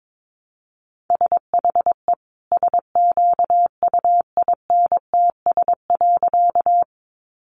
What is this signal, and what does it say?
Morse code, heard as a keyed tone, about 22 wpm, 700 Hz. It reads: SHE SQUINTS.
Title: Additional Easy Phrases 22wpm